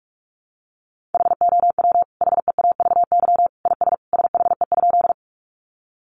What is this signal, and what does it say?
Morse code, heard as a keyed tone, about 44 wpm, 700 Hz. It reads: HOW HEAVY IS SHE?
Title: Sentences from Top 1100 Words 45wpm